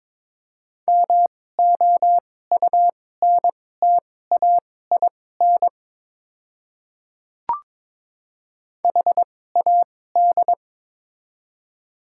Morse code, heard as a keyed tone, about 21 wpm, spaced as for 16 wpm, 700 Hz. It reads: MOUNTAIN HAD
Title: 2x Character Spacing - Top 300 Words - 22wpm